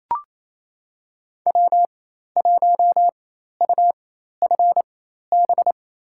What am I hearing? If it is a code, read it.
Morse code, heard as a keyed tone, about 27 wpm, spaced as for 13 wpm, 700 Hz. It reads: W1UFB
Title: USA Call Signs - 5 Characters - 4x Character Spacing - 28wpm